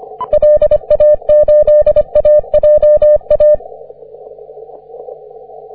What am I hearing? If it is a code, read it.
Morse code, heard as a keyed tone, about 22 wpm, 600 Hz. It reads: LA8AJA
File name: 20m cw